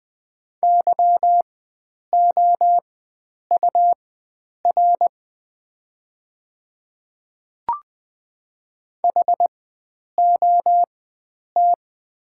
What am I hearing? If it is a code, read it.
Morse code, heard as a keyed tone, about 20 wpm, spaced as for 9 wpm, 700 Hz. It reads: YOUR HOT
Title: 4x Character Spacing - Top 100 Words - 20wpm